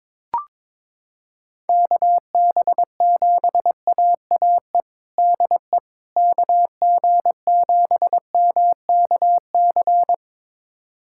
Morse code, heard as a keyed tone, about 22 wpm, 700 Hz. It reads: KB7AAE DE KG7MKC
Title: USA Callsign DE Callsign - 6 Characters - 22wpm